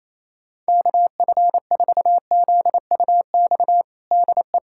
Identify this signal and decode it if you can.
Morse code, heard as a keyed tone, about 28 wpm, 700 Hz. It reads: KF4ZUX DE
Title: USA Callsign DE Callsign - 6 Characters - 28wpm